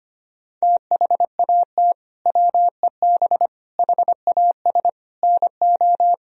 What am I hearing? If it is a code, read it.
Morse code, heard as a keyed tone, about 25 wpm, 700 Hz. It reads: THAT WEB HAS NO
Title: Sentences from Top 1000 Words - Review Part 4 - 25wpm